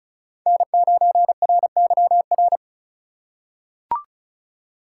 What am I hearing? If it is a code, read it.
Morse code, heard as a keyed tone, about 36 wpm, 700 Hz. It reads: N9RYR
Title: USA Call Signs - 3x Character Spacing - 35wpm